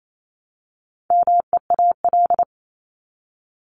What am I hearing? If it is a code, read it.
Morse code, heard as a keyed tone, about 28 wpm, 700 Hz. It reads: MEAL
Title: Top 1751-2000 Words 28wpm